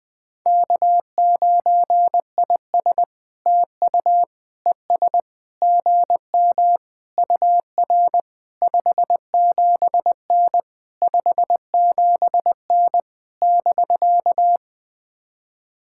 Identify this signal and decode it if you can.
Morse code, heard as a keyed tone, about 20 wpm, 700 Hz. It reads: K9IS TU ES GM UR 57N 57N <BK>
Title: POTA QSO Practice - 20wpm